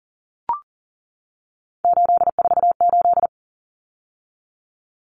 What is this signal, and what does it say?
Morse code, heard as a keyed tone, about 40 wpm, 700 Hz. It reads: 848
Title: Numbers - 3-Digit 40wpm